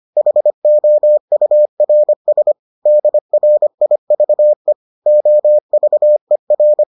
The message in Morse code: HOURS DRIVE OVER

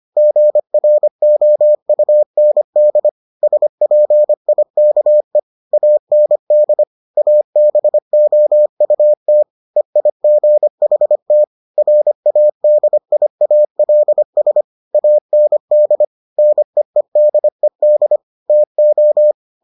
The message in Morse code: GROUND SPIKE AND ABOUT EIGHT RADIALS AND NEEDED TO

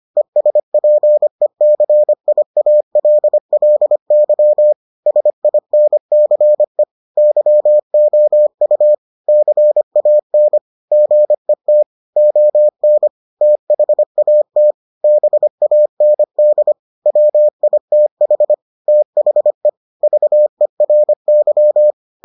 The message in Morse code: ESPECIALLY SINCE YOU CAN GET ON THAT BAND WITH THE VERY

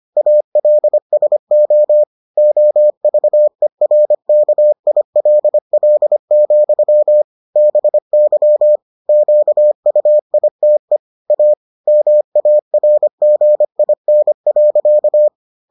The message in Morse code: ALSO OVERKILL, BY QUITE A MARGIN.

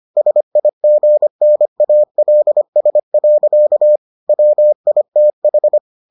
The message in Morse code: SIGNALS. WITH

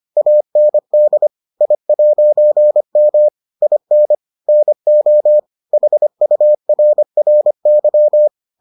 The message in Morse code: AND I'M IN NO HURRY

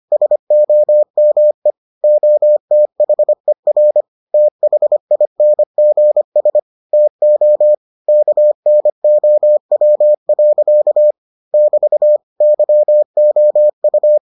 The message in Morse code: SOME OTHER THINGS TO KNOW. <BT> YOU

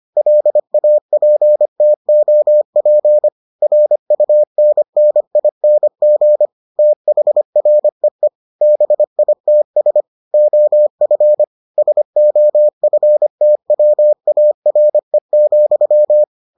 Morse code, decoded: LAPTOP RUNNING THREE BITS OF SOFTWARE,

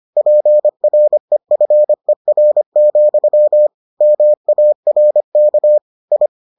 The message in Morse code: PREFER, MARK I

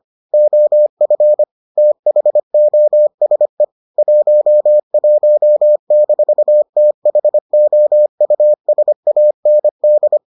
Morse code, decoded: OF THOSE 11-THOUSAND

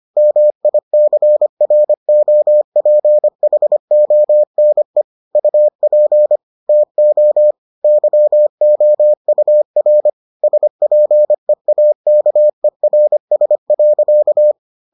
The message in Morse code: MICROPHONE UP TO YOUR SPEAKERS.

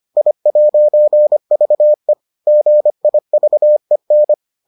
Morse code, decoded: I'VE GIVEN

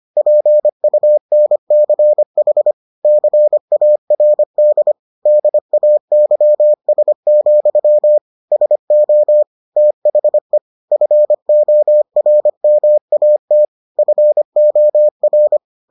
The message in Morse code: PUNCH CARD DAYS, SO THE FORMAT FOR

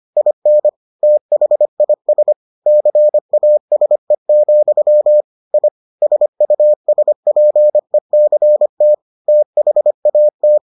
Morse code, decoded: IN THIS CASE, I SUSPECT THAT

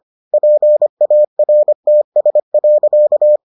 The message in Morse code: PARTS.

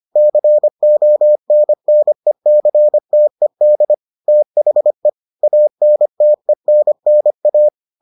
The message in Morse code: CONNECTED THE ANTENNA

